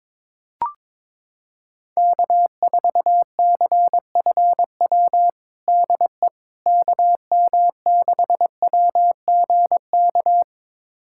Morse code, heard as a keyed tone, about 21 wpm, 700 Hz. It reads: K4CFW DE KM6WGK
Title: USA Callsign DE Callsign 22wpm